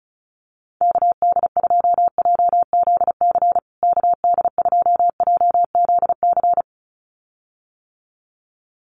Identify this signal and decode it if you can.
Morse code, heard as a keyed tone, about 34 wpm, 700 Hz. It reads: KD2JZC KD2JZC
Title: USA Call Signs Repeated 2x 35wpm